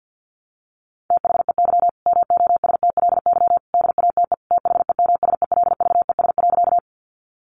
Morse code, heard as a keyed tone, about 50 wpm, 700 Hz. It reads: THEY MOSTLY DATE THEMSELVES.
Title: Sentences from Top 1300 Words 50wpm